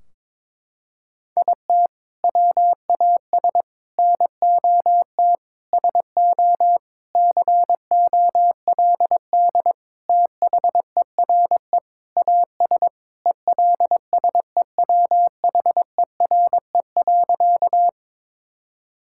Morse code, heard as a keyed tone, about 22 wpm, 700 Hz. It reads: IT WAS NOT SO COLD THERE AS ELSEWHERE.
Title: Sentences from Top 1600 Words 22wpm